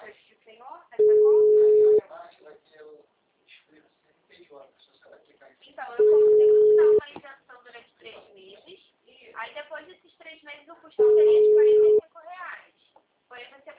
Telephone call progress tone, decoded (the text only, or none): ringback tone